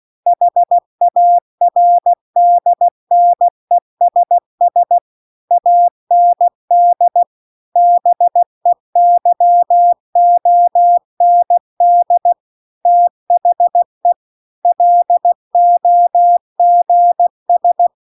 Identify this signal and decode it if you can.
Morse code, heard as a keyed tone, about 16 wpm, 700 Hz. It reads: HARDNESS AND BEYOND THE LOGS
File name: CW-The-Big-Twoheaded-River-2-16WPM